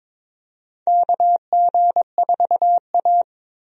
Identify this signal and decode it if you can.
Morse code, heard as a keyed tone, about 21 wpm, 700 Hz. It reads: KG4A
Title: USA Call Signs - 4 Characters - Repeated 2x - 22wpm